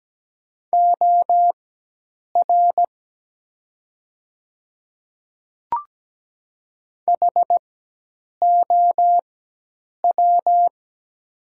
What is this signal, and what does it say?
Morse code, heard as a keyed tone, about 17 wpm, spaced as for 8 wpm, 700 Hz. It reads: OR HOW
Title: 4x Character Spacing - Top 300 Words - 17wpm